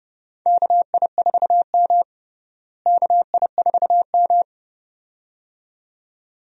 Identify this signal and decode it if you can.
Morse code, heard as a keyed tone, about 30 wpm, 700 Hz. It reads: KI4M KI4M
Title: USA Call Signs - 4 Characters - Repeated 2x - 30wpm